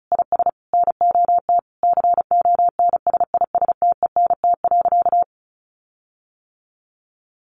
Morse code, heard as a keyed tone, about 34 wpm, 700 Hz. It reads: IS NOT CONSISTENT.
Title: Sentences from Top 1500 Words 35wpm